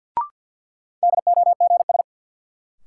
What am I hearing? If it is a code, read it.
Morse code, heard as a keyed tone, about 52 wpm, 700 Hz. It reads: DOGS
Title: Sentences from Top 1600 Words - New Word Drill - 50wpm